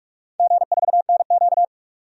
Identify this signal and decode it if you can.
Morse code, heard as a keyed tone, about 42 wpm, 700 Hz. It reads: G4NQ
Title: DXCC-100 Call Signs - 4 Characters - 45wpm